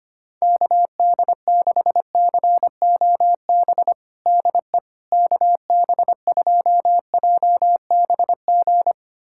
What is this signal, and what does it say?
Morse code, heard as a keyed tone, about 25 wpm, 700 Hz. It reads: KD6COB DE KB2JBG
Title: USA Callsign DE Callsign - 6 Characters - 25wpm